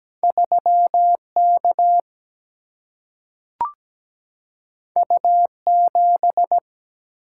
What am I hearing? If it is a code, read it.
Morse code, heard as a keyed tone, about 17 wpm, 700 Hz. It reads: 3 K U 7
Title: Single-Letter Single-Number 17wpm